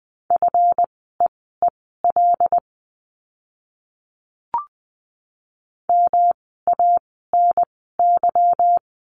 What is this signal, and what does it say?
Morse code, heard as a keyed tone, about 20 wpm, spaced as for 14 wpm, 700 Hz. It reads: FEEL MANY
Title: 244 Lesson 39f - Letter z - Words 20wpm